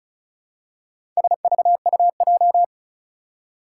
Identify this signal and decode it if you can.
Morse code, heard as a keyed tone, about 36 wpm, 700 Hz. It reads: SVUJ
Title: Four Challenging Letter-Number Combinations 35wpm